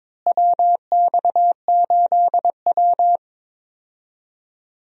Morse code, heard as a keyed tone, about 22 wpm, 700 Hz. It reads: WX8W
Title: USA Call Signs - 4 Characters - Repeated 3x - 22wpm